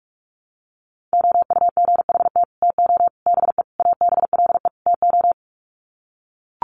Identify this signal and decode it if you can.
Morse code, heard as a keyed tone, about 44 wpm, 700 Hz. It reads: OUGHT TO BE ABLE TO
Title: Sets of 5 Words Encore2 45wpm